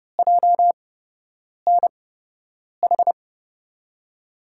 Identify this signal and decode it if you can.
Morse code, heard as a keyed tone, about 32 wpm, spaced as for 8 wpm, 700 Hz. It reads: JNH
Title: USA Call Signs - 6 Characters - 8x Character Spacing - 30wpm